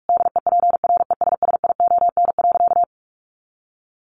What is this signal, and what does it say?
Morse code, heard as a keyed tone, about 44 wpm, 700 Hz. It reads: DEPRESSION.
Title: Sentences from Top 1300 Words 45wpm